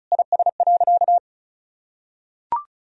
Morse code, heard as a keyed tone, about 36 wpm, 700 Hz. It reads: IS.
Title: Sentences from Top 1000 Words - Review Part 4 - 35wpm